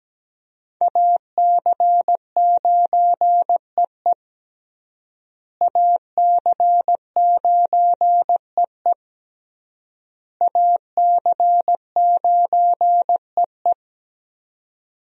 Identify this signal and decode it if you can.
Morse code, heard as a keyed tone, about 17 wpm, 700 Hz. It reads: AC9EE AC9EE AC9EE
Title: USA Call Signs - 5 Characters - Repeated 3x - 17wpm